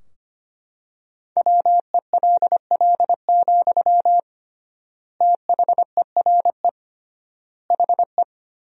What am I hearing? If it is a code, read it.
Morse code, heard as a keyed tone, about 25 wpm, 700 Hz. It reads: WELL, THERE HE
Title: 3x Word-Spacing - Sentences from Top 500 Words - 25wpm